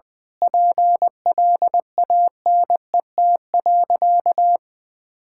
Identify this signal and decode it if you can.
Morse code, heard as a keyed tone, about 20 wpm, 700 Hz. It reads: PLANET.
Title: Difficult Sentences with Hints - 20wpm